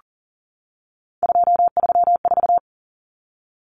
Morse code, heard as a keyed tone, about 40 wpm, 700 Hz. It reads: 234
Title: Numbers - 3-Digit 40wpm